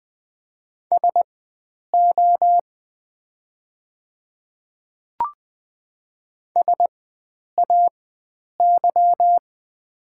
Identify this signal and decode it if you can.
Morse code, heard as a keyed tone, about 20 wpm, spaced as for 9 wpm, 700 Hz. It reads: SO SAY
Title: 4x Character Spacing - Top 300 Words - 20wpm